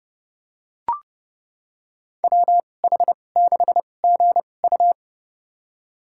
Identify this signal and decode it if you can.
Morse code, heard as a keyed tone, about 30 wpm, spaced as for 22 wpm, 700 Hz. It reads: WH6GU
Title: USA Call Signs - 5 Characters - 2x Character Spacing - 30wpm